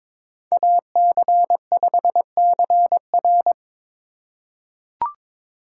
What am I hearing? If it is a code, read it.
Morse code, heard as a keyed tone, about 23 wpm, 700 Hz. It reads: AC5CR
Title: USA Call Signs - 5 Characters - 3x Character Spacing - 22wpm